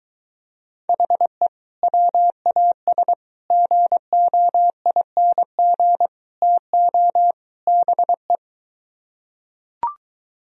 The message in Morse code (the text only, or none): HE WAS GOING TO BE